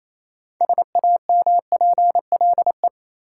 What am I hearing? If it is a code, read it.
Morse code, heard as a keyed tone, about 28 wpm, 700 Hz. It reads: SAMPLE
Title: Six Letter Words - 28wpm